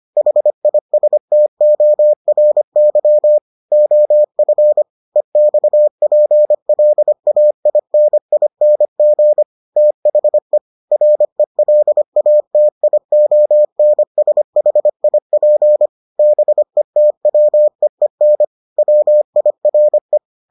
Morse code, heard as a keyed tone, about 25 wpm, 600 Hz. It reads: HISTORY OF EXPLAINING THE RELATIONSHIP BETWEEN WIRE